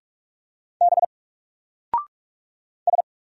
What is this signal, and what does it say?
Morse code, heard as a keyed tone, about 44 wpm, 700 Hz. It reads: B S
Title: Single Character (Letter:Number:Punctuation:Prosign) Alt 45wpm